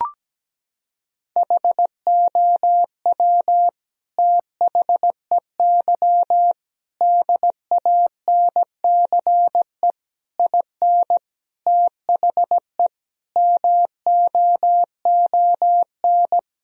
Morse code, heard as a keyed tone, about 17 wpm, 700 Hz. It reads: HOW THEY DANCE IN THE MOON
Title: Sentences from Top 1000 Words - Review Part 4 - 17wpm